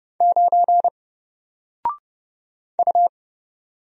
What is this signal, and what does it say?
Morse code, heard as a keyed tone, about 30 wpm, 700 Hz. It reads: 9 U
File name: Single-Letter-Number-30wpm